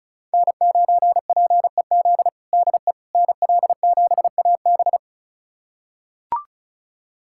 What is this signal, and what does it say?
Morse code, heard as a keyed tone, about 34 wpm, 700 Hz. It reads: N9PEZ DE NL7AB
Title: USA Callsign DE Callsign - 5 Characters - 35wpm